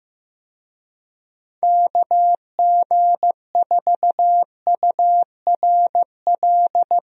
Morse code, heard as a keyed tone, about 15 wpm, 700 Hz. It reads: KG4URL
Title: USA Call Signs - 6 Characters - Repeated 2x - 15wpm